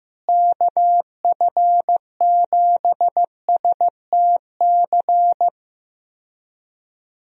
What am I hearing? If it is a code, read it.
Morse code, heard as a keyed tone, about 15 wpm, 700 Hz. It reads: KF7STC
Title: USA Call Signs - 6 Characters - Repeated 3x - 15wpm